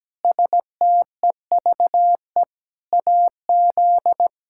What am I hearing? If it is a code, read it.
Morse code, heard as a keyed tone, about 17 wpm, 700 Hz. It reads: STEVE AZ
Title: K1USN SST QSO Practice - 17wpm